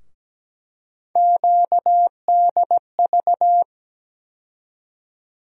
Morse code, heard as a keyed tone, about 17 wpm, 700 Hz. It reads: QDV
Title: USA Call Sign Postfixes - 3 Letters 17wpm